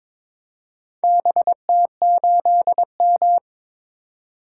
Morse code, heard as a keyed tone, about 22 wpm, 700 Hz. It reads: BT8M
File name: practice / 22wpm / Four Letter-Number Combinations - 22wpm